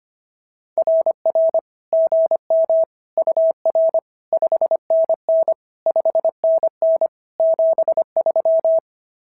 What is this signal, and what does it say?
Morse code, heard as a keyed tone, about 25 wpm, 690 Hz. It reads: RR GM UR 5NN 5NN 73
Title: SOTA QSO Practice 25wpm